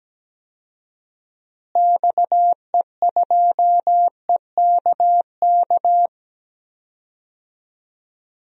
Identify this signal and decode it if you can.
Morse code, heard as a keyed tone, about 17 wpm, 700 Hz. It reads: XE2EKK
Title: DXCC-100 Call Signs - 6 Characters - Repeated 2x - 17wpm